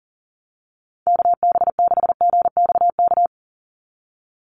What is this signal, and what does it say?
Morse code, heard as a keyed tone, about 40 wpm, 700 Hz. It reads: KB6GXK
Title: USA Call Signs 40wpm